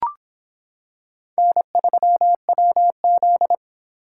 Morse code, heard as a keyed tone, about 26 wpm, 700 Hz. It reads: N3WZ